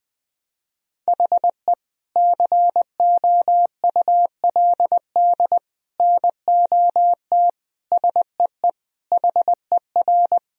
Morse code, heard as a keyed tone, about 20 wpm, 700 Hz. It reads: HE COULD NOT SEE HER
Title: Sentences from Top 1000 Words - Review Part 2 - 20wpm